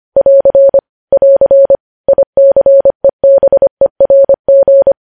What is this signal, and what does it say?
Morse code, heard as a keyed tone, about 25 wpm, 550 Hz. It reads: <AR> <AR> ICEBERG
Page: News Headlines in Morse Code at 25 WPM